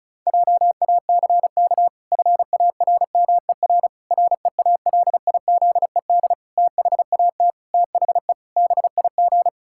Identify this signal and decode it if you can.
Morse code, heard as a keyed tone, about 36 wpm, 700 Hz. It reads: JACK FARMER REALIZED THAT THE BIG
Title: Pangram Challenge v2 - 35wpm